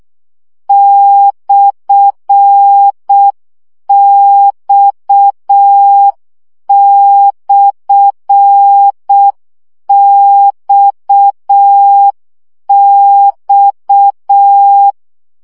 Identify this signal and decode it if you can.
Morse code, heard as a keyed tone, about 6 wpm, 800 Hz. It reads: /X/XX
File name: CW gyakorlas x per 5 baud